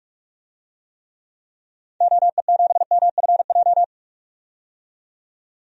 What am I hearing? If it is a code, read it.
Morse code, heard as a keyed tone, about 44 wpm, 700 Hz. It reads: OE7MFJ
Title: DXCC-100 Call Signs - 6 Characters - 45wpm